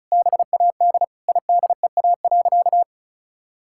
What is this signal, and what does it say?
Morse code, heard as a keyed tone, about 36 wpm, 700 Hz. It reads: BAD IDEA.
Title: Sentences from Top 1000 Words - Review Part 5 - 35wpm